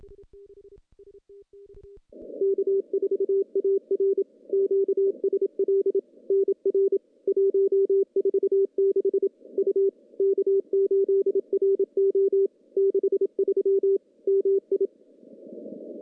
Morse code, heard as a keyed tone, about 27 wpm, 390 Hz. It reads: K4AR QSL NR 146 U K8RO 63 MI
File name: CW-SS-k5tr